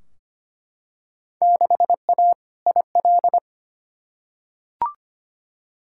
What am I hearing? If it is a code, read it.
Morse code, heard as a keyed tone, about 25 wpm, 700 Hz. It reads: 6A IL
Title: Field Day Exchanges - 25wpm